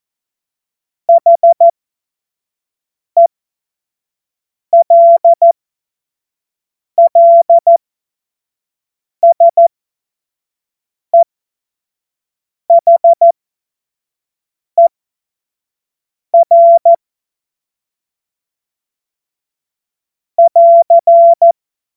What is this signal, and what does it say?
Morse code, heard as a keyed tone, about 14 wpm, spaced as for 5 wpm, 680 Hz. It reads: HELLSEHER <AR>
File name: Morsekurs / Witze / Witz 26 März 2024 25 F70